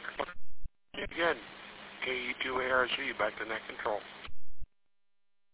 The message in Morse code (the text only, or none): none